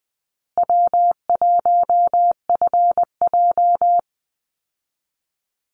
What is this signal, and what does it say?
Morse code, heard as a keyed tone, about 20 wpm, 700 Hz. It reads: W1FJ
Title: Easy Call Signs Repeated 3x - 20wpm